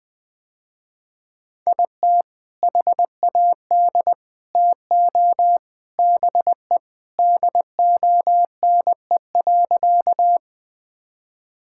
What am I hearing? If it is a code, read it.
Morse code, heard as a keyed tone, about 20 wpm, 700 Hz. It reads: IT HAD TO BE DONE.
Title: Sentences with 3-Word Phrases - Spoken Beginning - 20wpm